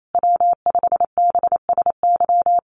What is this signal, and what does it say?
Morse code, heard as a keyed tone, about 28 wpm, 700 Hz. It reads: W5BSY
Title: USA Call Signs Repeated 2x 28wpm